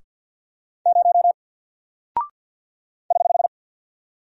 Morse code, heard as a keyed tone, about 48 wpm, 700 Hz. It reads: 0 <HH>
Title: Single Character (Letter:Number:Punctuation:Prosign) Alt 50wpm